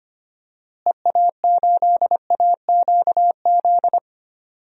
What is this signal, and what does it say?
Morse code, heard as a keyed tone, about 25 wpm, 700 Hz. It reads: EA8AQZ
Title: DXCC-100 Call Signs - 6 Characters - Repeated 2x - 25wpm